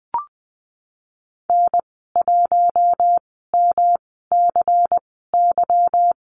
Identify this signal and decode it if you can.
Morse code, heard as a keyed tone, about 19 wpm, spaced as for 14 wpm, 700 Hz. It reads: N1MCY
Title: 147 Lesson 24g - Letter y - Call Signs 20wpm